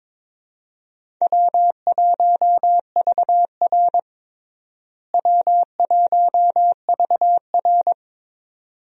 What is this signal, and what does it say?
Morse code, heard as a keyed tone, about 21 wpm, 700 Hz. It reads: W1VR W1VR
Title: USA Call Signs - 4 Characters - Repeated 2x - 22wpm